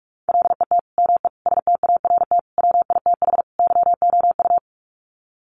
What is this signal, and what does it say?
Morse code, heard as a keyed tone, about 44 wpm, 700 Hz. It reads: LET ME START WITH YOU
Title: Sets of 5 Words Encore 45wpm